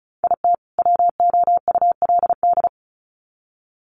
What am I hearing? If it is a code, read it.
Morse code, heard as a keyed tone, about 34 wpm, 700 Hz. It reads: IT WOULD
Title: Sets of 2 Words 35wpm